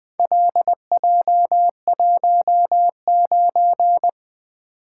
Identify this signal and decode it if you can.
Morse code, heard as a keyed tone, about 20 wpm, 700 Hz. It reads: LJ19
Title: Four Letter-Number Combinations - 20wpm